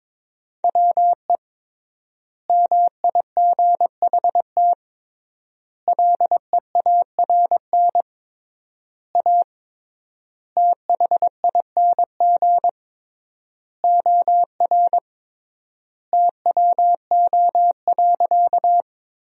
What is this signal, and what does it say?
Morse code, heard as a keyed tone, about 22 wpm, 700 Hz. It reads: WE MIGHT LEARN A THING OR TWO.
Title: 3x Word-Spacing - Sentences from Top 300 Words - 22wpm